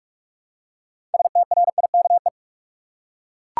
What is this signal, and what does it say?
Morse code, heard as a keyed tone, about 41 wpm, 700 Hz. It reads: STRIKE
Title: Six Letter Words - 45wpm